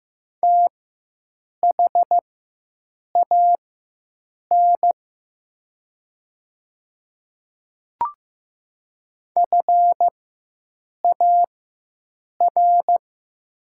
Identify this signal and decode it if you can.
Morse code, heard as a keyed tone, about 15 wpm, spaced as for 7 wpm, 700 Hz. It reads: THAN FAR
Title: 4x Character Spacing - Top 300 Words - 15wpm